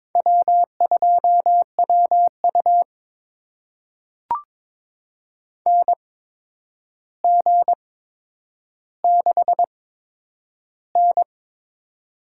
Morse code, heard as a keyed tone, about 22 wpm, spaced as for 9 wpm, 700 Hz. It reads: W2WU NG6N
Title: USA Call Signs - 8x Character Spacing - 22wpm